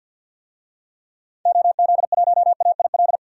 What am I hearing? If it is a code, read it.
Morse code, heard as a keyed tone, about 53 wpm, 700 Hz. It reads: OZ1AIL
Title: DXCC-100 Call Signs - 6 Characters - 50wpm